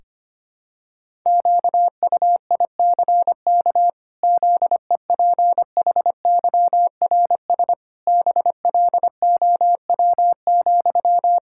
Morse code, heard as a keyed tone, about 25 wpm, 700 Hz. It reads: QUICK ZEPHYRS BLOW,
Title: Pangram Challenge v2 - 25wpm